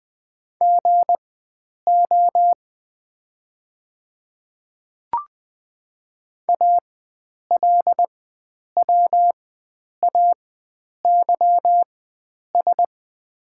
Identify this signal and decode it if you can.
Morse code, heard as a keyed tone, about 20 wpm, spaced as for 9 wpm, 700 Hz. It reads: GO ALWAYS
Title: 4x Character Spacing - Top 300 Words - 20wpm